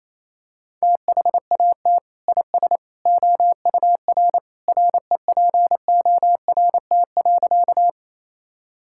Morse code, heard as a keyed tone, about 28 wpm, 700 Hz. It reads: THAT IS OUR REPORT.
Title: Sentences from Top 1000 Words - Review Part 6 - 28wpm